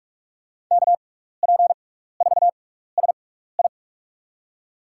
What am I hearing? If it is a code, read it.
Morse code, heard as a keyed tone, about 42 wpm, spaced as for 15 wpm, 700 Hz. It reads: KP4SI
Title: USA Call Signs - 5 Characters - 6x Character Spacing - 45wpm